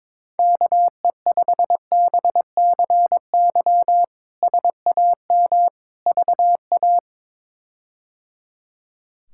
Morse code, heard as a keyed tone, about 21 wpm, 700 Hz. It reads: KE5BCY SAM VA
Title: K1USN SST QSO Practice - 22wpm